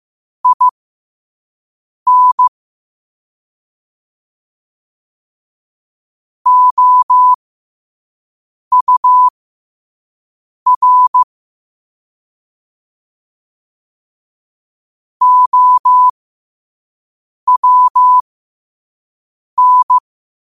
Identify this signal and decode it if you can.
Morse code, heard as a keyed tone, about 15 wpm, spaced as for 5 wpm, 1000 Hz. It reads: IN OUR OWN
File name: qotdfiles / 5wpm